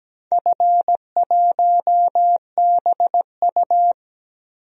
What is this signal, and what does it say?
Morse code, heard as a keyed tone, about 17 wpm, 700 Hz. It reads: F1BU
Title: DXCC-100 Call Signs - 4 Characters - 17wpm